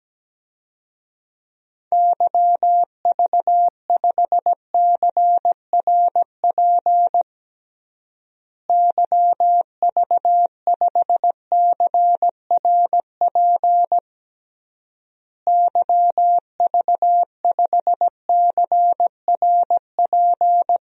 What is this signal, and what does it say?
Morse code, heard as a keyed tone, about 17 wpm, 700 Hz. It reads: YV5CRP YV5CRP YV5CRP
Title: DXCC-100 Call Signs - 6 Characters - Repeated 3x - 17wpm